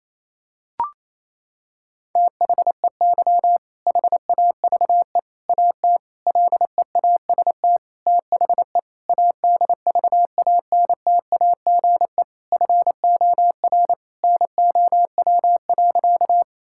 Morse code, heard as a keyed tone, about 27 wpm, 700 Hz. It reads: THEY HAVE AT LEAST THE ADVANTAGE FOR NOW.
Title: Sentences from Top 1000 Words - Review Part 6 - 28wpm